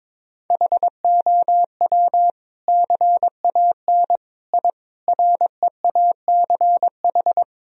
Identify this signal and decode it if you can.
Morse code, heard as a keyed tone, about 21 wpm, 700 Hz. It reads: HOW CAN I REACH
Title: Sentences from Top 1000 Words - Review Part 2 - 22wpm